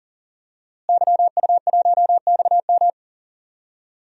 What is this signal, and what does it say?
Morse code, heard as a keyed tone, about 40 wpm, 700 Hz. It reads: YU1XM
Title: DXCC-100 Call Signs - 5 Characters - Repeated 3x - 40wpm